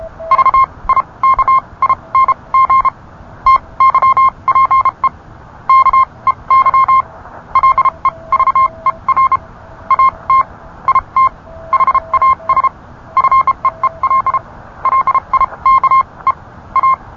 Morse code, heard as a keyed tone, about 31 wpm, spaced as for 25 wpm, 1060 Hz. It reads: VIKING TYPE KEY LEVER AT IT HAS FEEL LIKE A